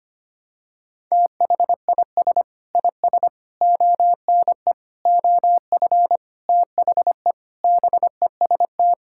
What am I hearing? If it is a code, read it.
Morse code, heard as a keyed tone, about 25 wpm, 700 Hz. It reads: THIS IS ONE OF THE BEST
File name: practice / 25wpm / Sentences with 4-Word Phrases - 25wpm